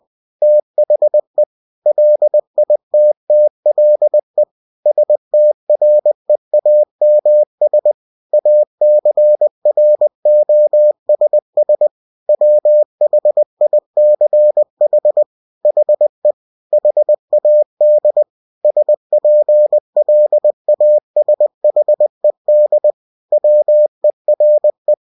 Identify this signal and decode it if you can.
Morse code, heard as a keyed tone, about 20 wpm, 600 Hz. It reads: THE LITTLE STREAMS ACROSS WHICH HE HAD SPLASHED WERE
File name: cw-books / winnie-the-pooh / winnie-20wpm-page-0091